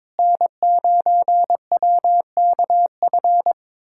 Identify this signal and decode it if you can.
Morse code, heard as a keyed tone, about 21 wpm, 700 Hz. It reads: N9WKF
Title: USA Callsign DE Callsign - 5 Characters - 22wpm